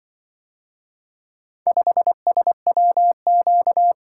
Morse code, heard as a keyed tone, about 23 wpm, 700 Hz. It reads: 5SWQ
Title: Speed-Racing - Four Letter-Number Combinations - 20wpm